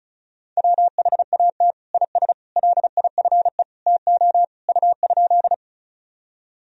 Morse code, heard as a keyed tone, about 36 wpm, 700 Hz. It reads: WHAT IS LIFE TO U?
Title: Sentences from Top 300 Words and Abbreviations 35wpm